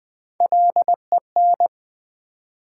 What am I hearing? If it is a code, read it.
Morse code, heard as a keyed tone, about 20 wpm, 700 Hz. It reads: LEN
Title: Build A Word - 1 Letter At A Time - 4-Letter Words - 20wpm